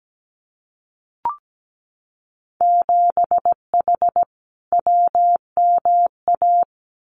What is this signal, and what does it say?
Morse code, heard as a keyed tone, about 17 wpm, 700 Hz. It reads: 7H WMA
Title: Winter Field Day Exchanges Encore 17wpm